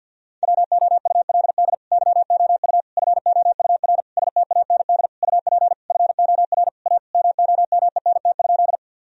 Morse code, heard as a keyed tone, about 46 wpm, 700 Hz. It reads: WOULD YOU FOUR STAND UP FOR A MOMENT?
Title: Sentences from Top 1000 Words - Review Part 1 - 50wpm